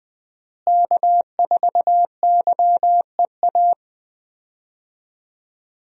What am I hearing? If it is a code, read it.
Morse code, heard as a keyed tone, about 20 wpm, 700 Hz. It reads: K4YEA
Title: USA Call Signs - 5 Characters - Repeated 3x - 20wpm